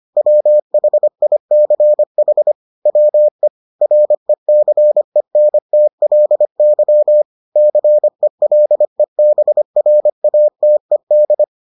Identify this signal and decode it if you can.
Morse code, heard as a keyed tone, about 25 wpm, 600 Hz. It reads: WHICH WE RECENTLY CELEBRATED